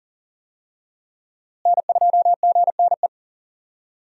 Morse code, heard as a keyed tone, about 40 wpm, 700 Hz. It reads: N2GNE
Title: USA Call Signs - 5 Characters - Repeated 3x - 40wpm